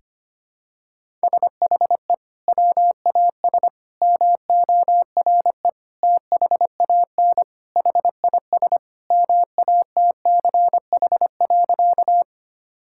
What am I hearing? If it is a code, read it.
Morse code, heard as a keyed tone, about 25 wpm, 700 Hz. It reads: SHE WAS MORE THAN HIS MATCH.
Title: Sentences from Top 1600 Words 25wpm